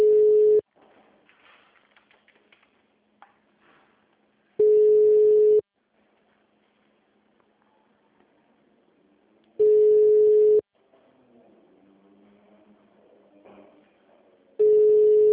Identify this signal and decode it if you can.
Ringback tone (Europe, 425 Hz).